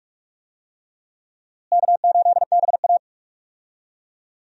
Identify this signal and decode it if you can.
Morse code, heard as a keyed tone, about 46 wpm, 700 Hz. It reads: K8BA
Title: USA Call Signs - 4 Characters - Repeated 3x - 45wpm